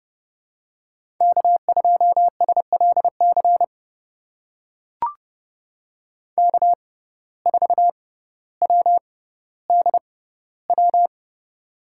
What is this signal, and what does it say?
Morse code, heard as a keyed tone, about 32 wpm, spaced as for 16 wpm, 700 Hz. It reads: K2SLC K4WDW
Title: USA Call Signs - 5 Characters - 6x Character Spacing - 30wpm